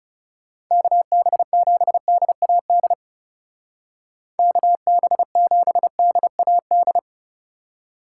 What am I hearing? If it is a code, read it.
Morse code, heard as a keyed tone, about 38 wpm, 700 Hz. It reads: KB7DAD KB7DAD
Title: Speed Racing - USA 2x3 Call Signs 30wpm